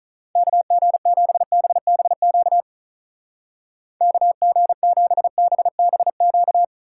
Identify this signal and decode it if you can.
Morse code, heard as a keyed tone, about 38 wpm, 700 Hz. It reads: KG7BBQ KG7BBQ
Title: Speed Racing - USA 2x3 Call Signs 35wpm